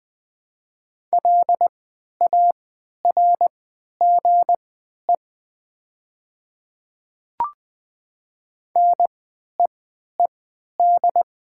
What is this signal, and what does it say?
Morse code, heard as a keyed tone, about 20 wpm, spaced as for 11 wpm, 690 Hz. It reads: LARGE NEED
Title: 3x Character Spacing - Top 200 Words - 20wpm